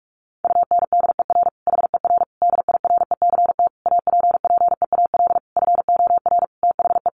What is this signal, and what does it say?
Morse code, heard as a keyed tone, about 44 wpm, 700 Hz. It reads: UNDER HER DIRECT APPEAL FOR THE
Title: Sentences from Top 1300 Words 45wpm